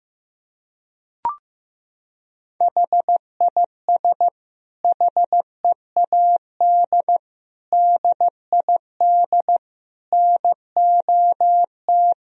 Morse code, heard as a keyed tone, about 15 wpm, 700 Hz. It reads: HIS HEAD DID NOT
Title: Sentences from Top 1000 Words - Review Part 2 - 15wpm